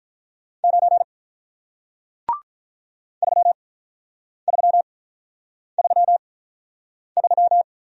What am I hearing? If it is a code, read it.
Morse code, heard as a keyed tone, about 47 wpm, spaced as for 9 wpm, 700 Hz. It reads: 9 3333